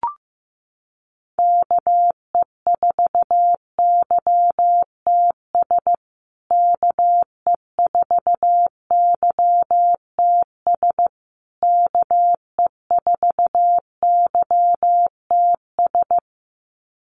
Morse code, heard as a keyed tone, about 14 wpm, 700 Hz. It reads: KE4YTS KE4YTS KE4YTS
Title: USA Call Signs Repeated 3x 15wpm